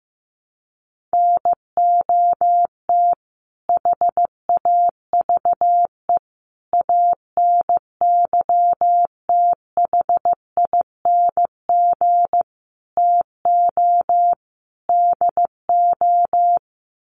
Morse code, heard as a keyed tone, about 15 wpm, 700 Hz. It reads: NOT HAVE ANYTHING TO DO
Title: Sets of 5 Words Encore 15wpm